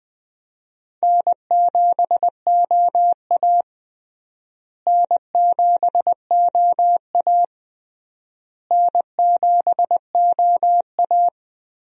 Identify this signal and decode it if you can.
Morse code, heard as a keyed tone, about 21 wpm, 700 Hz. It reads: N7OA N7OA N7OA
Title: USA Call Signs - 4 Characters - Repeated 3x - 20wpm